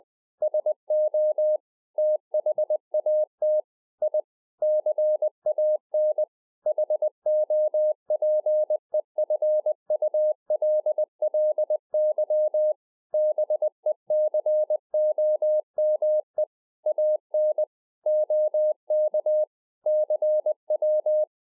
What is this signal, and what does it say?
Morse code, heard as a keyed tone, about 20 wpm, 610 Hz. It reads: SO THAT I CAN HOPEFULLY BECOME AN OK CW